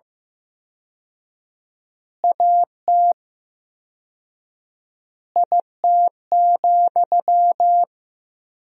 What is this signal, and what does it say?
Morse code, heard as a keyed tone, about 15 wpm, 700 Hz. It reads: AT IT,
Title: 4x Word-Spacing - Sentences from Top 200 Words - 15wpm